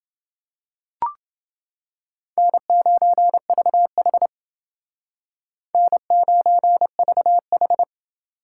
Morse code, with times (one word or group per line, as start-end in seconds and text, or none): 2.4-4.3 N9VH
5.7-7.8 N9VH